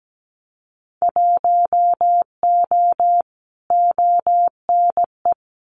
Morse code, heard as a keyed tone, about 17 wpm, 700 Hz. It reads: 1O ONE
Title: Winter Field Day Exchanges 17wpm